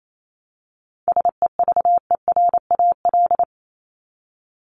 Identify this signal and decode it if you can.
Morse code, heard as a keyed tone, about 28 wpm, 700 Hz. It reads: SEVERAL
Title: Top 500 Words 28wpm